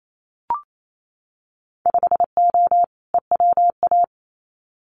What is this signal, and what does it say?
Morse code, heard as a keyed tone, about 28 wpm, 700 Hz. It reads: 5O EWA
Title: Winter Field Day Exchanges Encore 28wpm